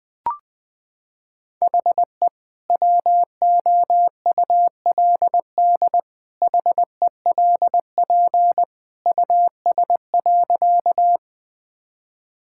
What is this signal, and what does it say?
Morse code, heard as a keyed tone, about 21 wpm, 700 Hz. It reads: HE WOULD HELP US.
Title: Sentences with 2-Word Phrases - 20wpm